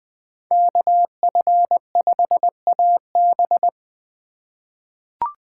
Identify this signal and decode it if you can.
Morse code, heard as a keyed tone, about 20 wpm, 700 Hz. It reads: KF5AB
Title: USA Call Signs - 2x Character Spacing - 20wpm